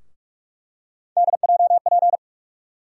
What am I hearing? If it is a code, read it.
Morse code, heard as a keyed tone, about 46 wpm, 700 Hz. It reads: DJP
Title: USA Call Sign Postfixes - 3 Letters 45wpm